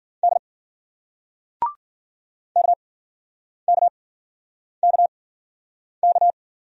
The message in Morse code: D KKKK

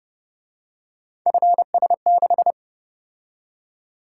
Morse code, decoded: FS6

